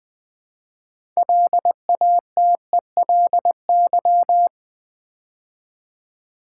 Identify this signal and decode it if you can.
Morse code, heard as a keyed tone, about 20 wpm, 690 Hz. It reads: LATELY
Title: Build A Word - 2 Letters At A Time - 6-Letter Words - 20wpm